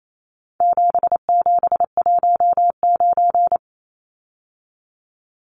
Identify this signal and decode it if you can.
Morse code, heard as a keyed tone, about 28 wpm, 700 Hz. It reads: 7719
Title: Numbers - 4-Digit 28wpm